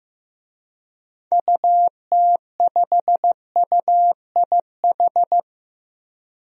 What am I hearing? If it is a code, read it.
Morse code, heard as a keyed tone, about 15 wpm, 700 Hz. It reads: UT5UIH
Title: DXCC-100 Call Signs - 6 Characters - Repeated 2x - 15wpm